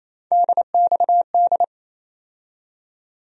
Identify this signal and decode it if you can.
Morse code, heard as a keyed tone, about 28 wpm, 700 Hz. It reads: DXD
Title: USA Call Sign Postfixes - 3 Letters 28wpm